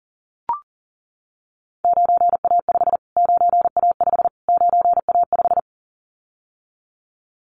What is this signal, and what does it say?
Morse code, heard as a keyed tone, about 39 wpm, 700 Hz. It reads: 9A5 9A5 9A5
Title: Callsign Prefixes for Top 25 DXCC - Repeated 3x - 40wpm